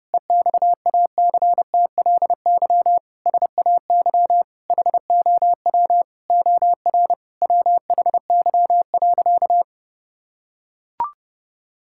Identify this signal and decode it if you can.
Morse code, heard as a keyed tone, about 30 wpm, 700 Hz. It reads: EXACTLY SAY HOW OR WHY.
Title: Sentences from Top 1000 Words - Review Part 6 - 30wpm